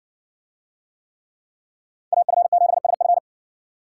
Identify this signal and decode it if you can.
Morse code, heard as a keyed tone, about 57 wpm, 700 Hz. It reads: AF7SF